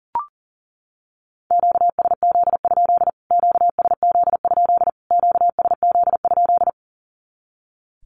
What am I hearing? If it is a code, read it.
Morse code, heard as a keyed tone, about 40 wpm, 700 Hz. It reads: QSZ? QSZ? QSZ?
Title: Q-Codes Repeated 3x 40wpm